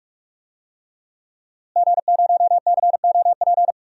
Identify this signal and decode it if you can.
Morse code, heard as a keyed tone, about 42 wpm, 700 Hz. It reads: G0COP
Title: DXCC-100 Call Signs - 5 Characters - Repeated 3x - 45wpm